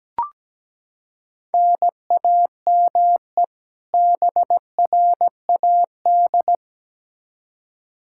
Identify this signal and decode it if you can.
Morse code, heard as a keyed tone, about 17 wpm, 700 Hz. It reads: NAME BRAD
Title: Practice QSO - Exchange 1 - Beginning - 17wpm